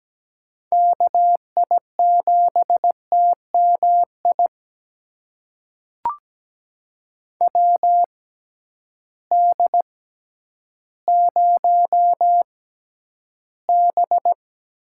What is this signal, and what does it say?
Morse code, heard as a keyed tone, about 17 wpm, spaced as for 10 wpm, 700 Hz. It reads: KI7TMI WD0B
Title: USA Call Signs - 6 Characters - 6x Character Spacing - 17wpm